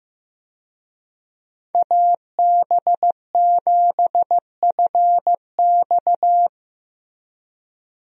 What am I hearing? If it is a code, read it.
Morse code, heard as a keyed tone, about 15 wpm, 700 Hz. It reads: AB7FX
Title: USA Call Signs - 5 Characters - Repeated 3x - 15wpm